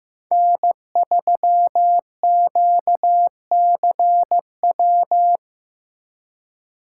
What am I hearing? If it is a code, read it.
Morse code, heard as a keyed tone, about 18 wpm, 700 Hz. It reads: N3QCW
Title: Speed Racing - USA 1x3 Call Signs 15wpm